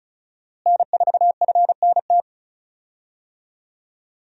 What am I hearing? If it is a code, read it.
Morse code, heard as a keyed tone, about 34 wpm, 700 Hz. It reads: N4FNT
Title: USA Call Signs - 5 Characters - Repeated 2x - 35wpm